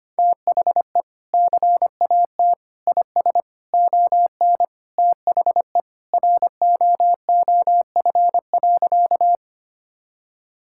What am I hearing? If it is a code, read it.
Morse code, heard as a keyed tone, about 25 wpm, 700 Hz. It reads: THE CAT IS ON THE ROOF.
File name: practice / 25wpm / Sentences with 3-Word Phrases - 25wpm